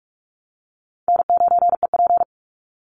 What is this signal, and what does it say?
Morse code, heard as a keyed tone, about 44 wpm, 700 Hz. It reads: N9EP
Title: Easy Call Signs Repeated 3x - 45wpm